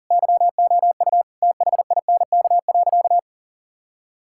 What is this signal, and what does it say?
Morse code, heard as a keyed tone, about 37 wpm, 700 Hz. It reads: YOU THINK.
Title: Sentences from Top 1000 Words - Review Part 6 - 40wpm